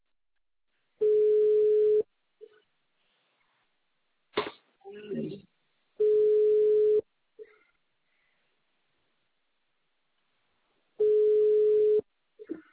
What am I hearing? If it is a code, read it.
Ringback tone (Europe, 425 Hz), with speech.